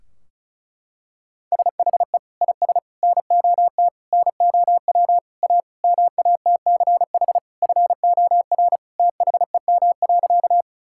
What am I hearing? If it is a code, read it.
Morse code, heard as a keyed tone, about 36 wpm, 700 Hz. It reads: SHE IS NOT NOW A MATCH FOR THEM.
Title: Sentences from Top 1600 Words 35wpm